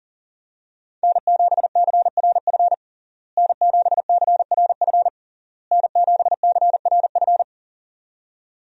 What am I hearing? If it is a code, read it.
Morse code, heard as a keyed tone, about 40 wpm, 700 Hz. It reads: N7CRF N7CRF N7CRF
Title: USA Call Signs - 5 Characters - Repeated 3x - 40wpm